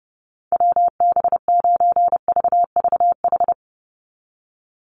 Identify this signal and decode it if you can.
Morse code, heard as a keyed tone, about 30 wpm, 700 Hz. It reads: WB9VVH
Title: USA Call Signs 30wpm